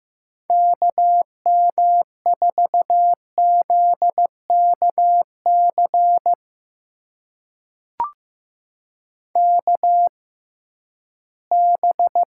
Morse code, heard as a keyed tone, about 15 wpm, spaced as for 11 wpm, 700 Hz. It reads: KM4ZKC KB
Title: USA Call Signs - 6 Characters - 6x Character Spacing - 15wpm